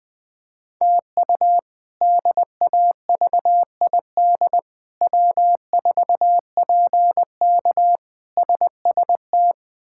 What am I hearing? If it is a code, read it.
Morse code, heard as a keyed tone, about 21 wpm, 700 Hz. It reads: TU DAVID W4PK SST
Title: K1USN SST QSO Practice - 20wpm